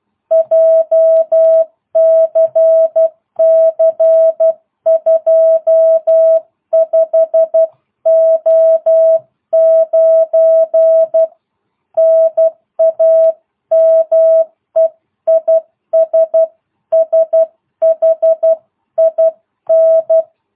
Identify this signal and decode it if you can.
Morse code, heard as a keyed tone, about 12 wpm, 660 Hz. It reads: JCC 25O9 NAME IS SHIN